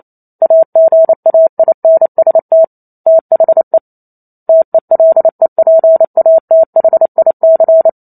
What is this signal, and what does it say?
Morse code, heard as a keyed tone, about 28 wpm, 650 Hz. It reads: AGAINST THE TELEPATHIC
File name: Chapter 2 - 28 WPM